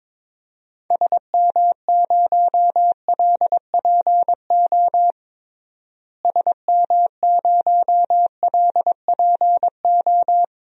Morse code, heard as a keyed tone, about 22 wpm, 700 Hz. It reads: SM0LPO SM0LPO
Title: DXCC-100 Call Signs - 6 Characters - Repeated 2x - 22wpm